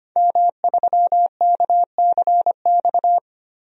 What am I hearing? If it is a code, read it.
Morse code, heard as a keyed tone, about 25 wpm, 700 Hz. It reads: M3KCX
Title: DXCC-100 Call Signs - 5 Characters - 25wpm